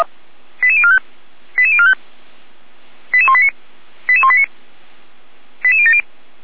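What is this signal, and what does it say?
Five-tone selective call (ZVEI), heard as quick runs of five tones, tones of about 80 ms. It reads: 88045 88045 88189 88189 88070
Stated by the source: ZVEI five-tone selective call received on an SDR, from the Feuerwehr Orlamünde fire brigade.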